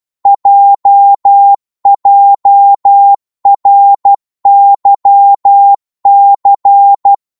Morse code, heard as a keyed tone, about 12 wpm, 800 Hz. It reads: JJRYC